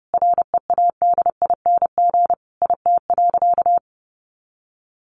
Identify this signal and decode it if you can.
Morse code, heard as a keyed tone, about 30 wpm, 700 Hz. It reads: READING IT.
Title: Sentences from Top 1200 Words 30wpm